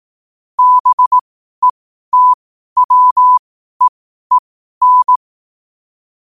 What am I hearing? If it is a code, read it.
Morse code, heard as a keyed tone, about 18 wpm, spaced as for 12 wpm, 1000 Hz. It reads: BETWEEN
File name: qotdfiles / 12wpm